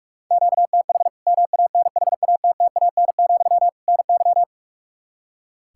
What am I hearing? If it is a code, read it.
Morse code, heard as a keyed tone, about 46 wpm, 700 Hz. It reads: QTH MANHATTAN, NY
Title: Practice QSO - Exchange 1 - Beginning - 45wpm